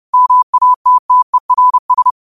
Morse code, heard as a keyed tone, about 30 wpm, 1000 Hz. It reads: MATTERS
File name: qotdfiles / 30wpm